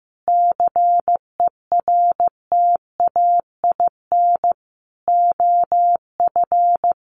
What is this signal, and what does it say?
Morse code, heard as a keyed tone, about 15 wpm, 700 Hz. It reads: CERTAIN OF
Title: Sentences from Top 1300 Words 15wpm